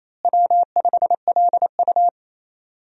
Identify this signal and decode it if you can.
Morse code, heard as a keyed tone, about 28 wpm, 700 Hz. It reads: W5LU
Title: USA Call Signs - 4 Characters - Repeated 3x - 28wpm